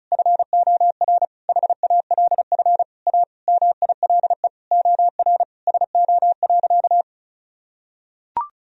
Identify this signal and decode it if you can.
Morse code, heard as a keyed tone, about 34 wpm, 690 Hz. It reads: FOR HALF A MILE OR SO.
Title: Sentences from Top 1000 Words - Review Part 1 - 35wpm